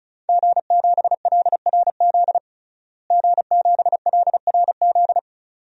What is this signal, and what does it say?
Morse code, heard as a keyed tone, about 36 wpm, 700 Hz. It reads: G7LRZ G7LRZ
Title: DXCC-100 Call Signs - 5 Characters - Repeated 2x - 35wpm